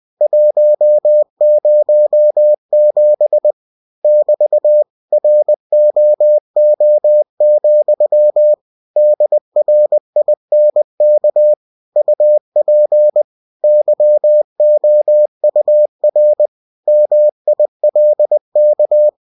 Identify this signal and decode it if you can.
Morse code, heard as a keyed tone, about 20 wpm, 600 Hz. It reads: 107 <BT> ROO, DRINK UP YOUR MILK
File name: cw-books / winnie-the-pooh / winnie-20wpm-page-0107